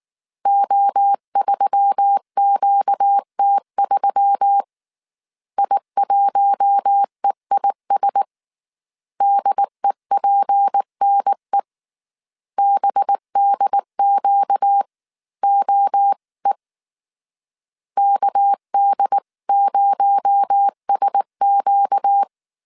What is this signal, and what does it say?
Morse code, heard as a keyed tone, about 19 wpm, 800 Hz. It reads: O3QT3 I1EIS DEPNE BDQOE KD0SQ